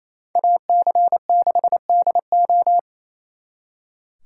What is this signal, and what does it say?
Morse code, heard as a keyed tone, about 32 wpm, 700 Hz. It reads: AC6DO
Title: Speed Racing - USA 2x2 Call Signs 28wpm